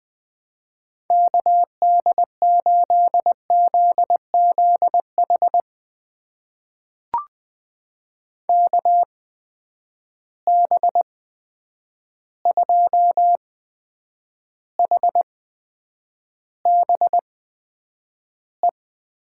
Morse code, heard as a keyed tone, about 20 wpm, spaced as for 9 wpm, 700 Hz. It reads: KD8ZZH KB2HBE
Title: USA Call Signs - 6 Characters - 8x Character Spacing - 20wpm